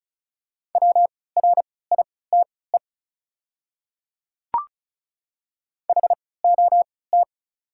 Morse code, heard as a keyed tone, about 34 wpm, spaced as for 20 wpm, 690 Hz. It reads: WRITE HOT
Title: 3x Character Spacing - Top 100 Words - 35wpm